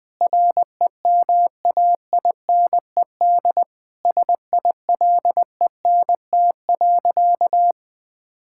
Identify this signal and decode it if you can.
Morse code, heard as a keyed tone, about 20 wpm, 700 Hz. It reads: REMAINED SILENT.
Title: Sentences with 7-Word Phrases - 20wpm